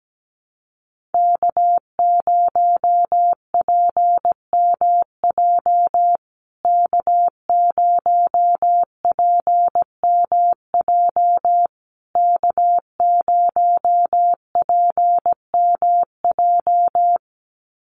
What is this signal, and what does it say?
Morse code, heard as a keyed tone, about 17 wpm, 700 Hz. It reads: K0PMJ K0PMJ K0PMJ
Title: USA Call Signs Repeated 3x 17wpm